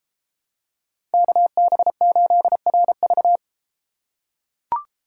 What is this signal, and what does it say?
Morse code, heard as a keyed tone, about 31 wpm, 700 Hz. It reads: KB8RV